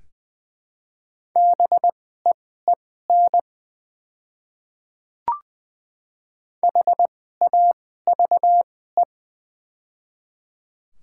Morse code, heard as a keyed tone, about 20 wpm, spaced as for 14 wpm, 700 Hz. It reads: BEEN HAVE
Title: 2x Character Spacing - Top 100 Words - 20wpm